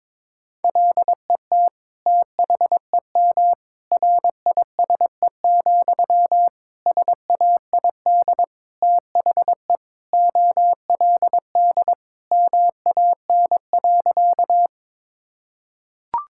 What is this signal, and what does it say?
Morse code, heard as a keyed tone, about 22 wpm, 700 Hz. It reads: LET THEM RISE, SAID THE OLD MAN.
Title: Sentences from Top 1000 Words - Review Part 6 - 22wpm